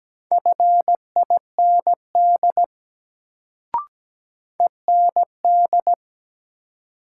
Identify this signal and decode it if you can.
Morse code, heard as a keyed tone, about 17 wpm, 700 Hz. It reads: FIND END
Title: Sets of 2 Words - New Word Drill - 17wpm